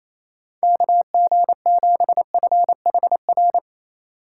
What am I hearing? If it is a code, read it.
Morse code, heard as a keyed tone, about 28 wpm, 700 Hz. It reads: KG7FHR
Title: USA Call Signs - 3x Character Spacing - 28wpm